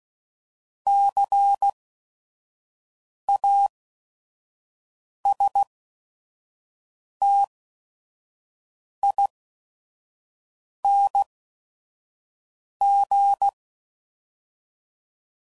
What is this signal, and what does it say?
Morse code, heard as a keyed tone, about 16 wpm, spaced as for 5 wpm, 800 Hz. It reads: CASTING
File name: Suffixes / Special ING's / VerySlowToFaster ING's